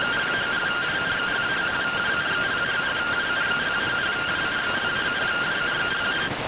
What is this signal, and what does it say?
Two-tone teleprinter signal, about 170 Hz shift.
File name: findings / SAGITAL nb203 fr61 3mm.gen-42 rtty rx